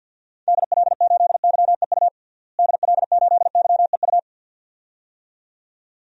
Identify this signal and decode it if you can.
Morse code, heard as a keyed tone, about 52 wpm, 700 Hz. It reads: DL8YEU DL8YEU
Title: DXCC-100 Call Signs - 6 Characters - Repeated 2x - 50wpm